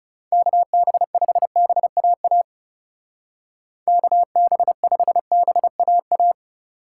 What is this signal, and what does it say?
Morse code, heard as a keyed tone, about 37 wpm, 700 Hz. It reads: KB5BAA KB5BAA
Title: Speed Racing - USA 2x3 Call Signs 30wpm